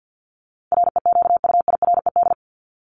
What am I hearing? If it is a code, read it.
Morse code, heard as a keyed tone, about 50 wpm, 700 Hz. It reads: REQUIRED
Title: ED Suffix - Top 100 Words with Base Words 50wpm